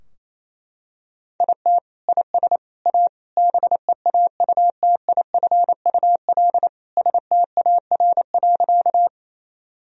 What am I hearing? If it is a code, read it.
Morse code, heard as a keyed tone, about 28 wpm, 700 Hz. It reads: IT IS A BEAUTIFUL STAR.
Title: Sentences from Top 1000 Words - Review Part 2 - 28wpm